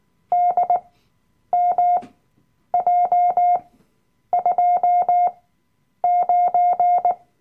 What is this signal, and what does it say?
Morse code, heard as a keyed tone, about 19 wpm, spaced as for 9 wpm, 710 Hz. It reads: DMJ29
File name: Workshop Dateien / CW-Online-Kurs / Lektion 007